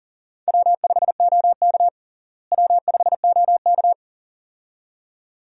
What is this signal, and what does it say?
Morse code, heard as a keyed tone, about 40 wpm, 700 Hz. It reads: W5OK W5OK
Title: USA Call Signs - 4 Characters - Repeated 2x - 40wpm